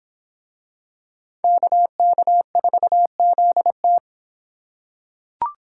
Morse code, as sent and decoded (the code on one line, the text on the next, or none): -.- -.- ....- --.. -
KK4ZT